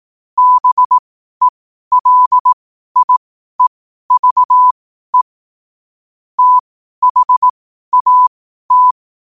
Morse code, heard as a keyed tone, about 18 wpm, spaced as for 12 wpm, 1000 Hz. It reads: BELIEVE THAT
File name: qotdfiles / 12wpm